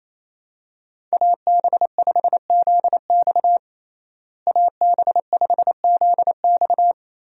Morse code, heard as a keyed tone, about 28 wpm, 700 Hz. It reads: AB5ZX AB5ZX
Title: USA Call Signs - 5 Characters - Repeated 3x - 28wpm